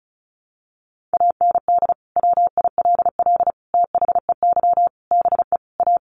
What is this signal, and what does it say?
Morse code, heard as a keyed tone, about 34 wpm, 700 Hz. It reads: AND WILL THEY BE A
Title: Sentences from Top 1400 Words 35wpm